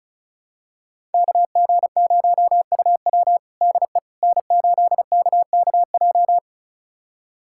Morse code, heard as a keyed tone, about 34 wpm, 700 Hz. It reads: KG0UW DE N8KKJ
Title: USA Callsign DE Callsign - 5 Characters - 35wpm